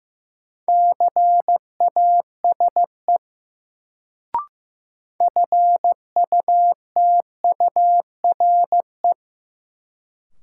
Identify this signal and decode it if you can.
Morse code, heard as a keyed tone, about 15 wpm, 700 Hz. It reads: CASE FUTURE
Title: Sentences from Top 700 Words - New Word Drill - 15wpm